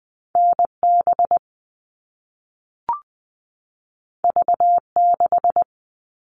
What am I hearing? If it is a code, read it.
Morse code, heard as a keyed tone, about 20 wpm, 700 Hz. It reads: NB V6
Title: All Cut Numbers - 2-Digit 20wpm